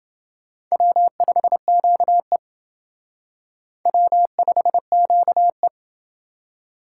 Morse code, heard as a keyed tone, about 28 wpm, 700 Hz. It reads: W5QE W5QE